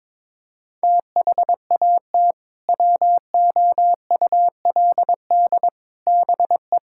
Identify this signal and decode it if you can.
Morse code, heard as a keyed tone, about 22 wpm, 700 Hz. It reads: THAT WOULD BE
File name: practice / 22wpm / Sentences with 2-Word Phrases - 22wpm